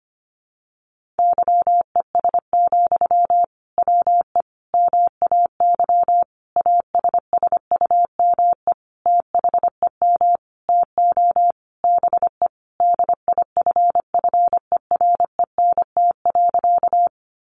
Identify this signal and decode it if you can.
Morse code, heard as a keyed tone, about 25 wpm, 700 Hz. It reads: YES, WE MAY ASSUME THEM TO BE DIFFERENT.
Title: Sentences from Top 1500 Words 25wpm